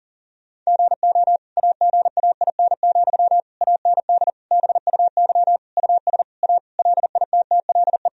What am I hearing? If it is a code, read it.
Morse code, heard as a keyed tone, about 40 wpm, 700 Hz. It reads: GO AGAIN, AND BUY US A LITTLE
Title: Sentences from Top 1000 Words - Review Part 4 - 40wpm